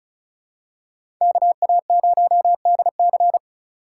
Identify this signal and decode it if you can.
Morse code, heard as a keyed tone, about 36 wpm, 700 Hz. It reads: KA0DC
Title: USA Call Signs - 2x Character Spacing - 35wpm